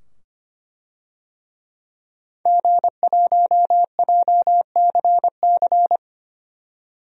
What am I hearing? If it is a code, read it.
Morse code, heard as a keyed tone, about 25 wpm, 700 Hz. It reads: G1JCC
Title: DXCC-100 Call Signs - 5 Characters - 25wpm